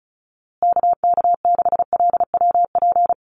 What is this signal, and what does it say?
Morse code, heard as a keyed tone, about 34 wpm, 700 Hz. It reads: KK6LWP
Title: USA Call Signs Repeated 2x 35wpm